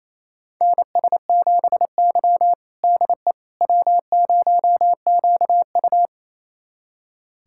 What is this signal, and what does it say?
Morse code, heard as a keyed tone, about 27 wpm, 700 Hz. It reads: NS7Y DE W0QU
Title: USA Callsign DE Callsign - 4 Characters - 28wpm